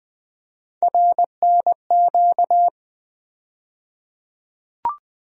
Morse code, heard as a keyed tone, about 21 wpm, 700 Hz. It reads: RNQ
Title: USA Call Sign Postfixes - 3 Letters 20wpm